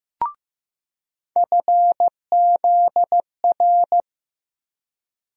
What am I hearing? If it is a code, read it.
Morse code, heard as a keyed tone, about 15 wpm, 700 Hz. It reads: FZR
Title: USA Call Sign Postfixes - 3 Letters 15wpm